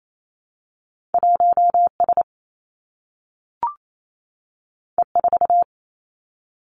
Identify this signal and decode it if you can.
Morse code, heard as a keyed tone, about 28 wpm, 700 Hz. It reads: 1 S E 4
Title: Two Challenging Letter-Number Combinations 28wpm